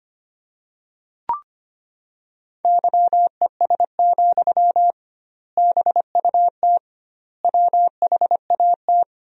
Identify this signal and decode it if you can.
Morse code, heard as a keyed tone, about 25 wpm, 700 Hz. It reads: YES, BUT WHAT
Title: 2x Word-Spacing - Sentences from Top 500 Words - 25wpm